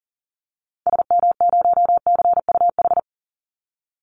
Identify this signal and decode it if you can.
Morse code, heard as a keyed tone, about 40 wpm, 700 Hz. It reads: SM0CUH
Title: Call Signs Repeated 3x - 40wpm